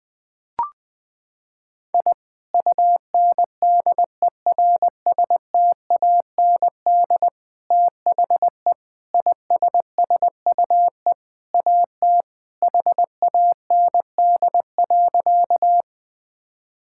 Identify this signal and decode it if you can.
Morse code, heard as a keyed tone, about 20 wpm, 700 Hz. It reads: I UNDERSTAND THE ISSUE AT HAND.
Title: Sentences from Top 1000 Words - Review Part 6 - 20wpm